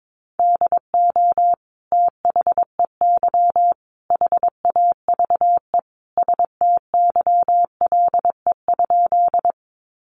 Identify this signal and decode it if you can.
Morse code, heard as a keyed tone, about 22 wpm, 700 Hz. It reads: DO THEY HAVE STYLE?
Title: Sentences from Top 800 Words 22wpm